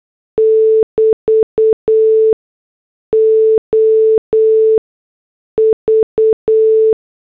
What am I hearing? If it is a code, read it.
Morse code, heard as a keyed tone, about 8 wpm, spaced as for 6 wpm, 440 Hz. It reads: <BT> OV